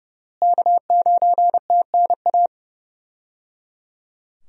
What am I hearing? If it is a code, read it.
Morse code, heard as a keyed tone, about 30 wpm, 700 Hz. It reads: K9TNA
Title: USA Call Signs - 5 Characters - Repeated 3x - 30wpm